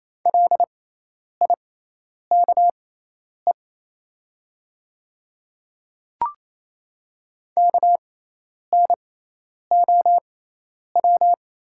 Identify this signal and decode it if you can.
Morse code, heard as a keyed tone, about 28 wpm, spaced as for 10 wpm, 700 Hz. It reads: LIKE KNOW
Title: 6x Character Spacing - Top 200 Words - 28wpm